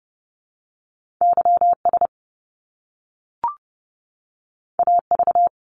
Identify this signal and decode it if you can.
Morse code, heard as a keyed tone, about 30 wpm, 700 Hz. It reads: YS AV
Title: Top Two Letter Combinations in Words 30wpm